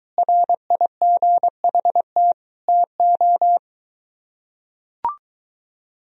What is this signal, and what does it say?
Morse code, heard as a keyed tone, about 23 wpm, 700 Hz. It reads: RIGHT TO